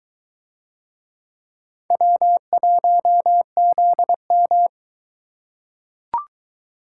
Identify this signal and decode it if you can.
Morse code, heard as a keyed tone, about 22 wpm, 700 Hz. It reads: W1ZM